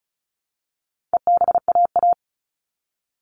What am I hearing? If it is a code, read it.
Morse code, heard as a keyed tone, about 34 wpm, 700 Hz. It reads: EBAA
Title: All Cut Numbers - 4-Digit 35wpm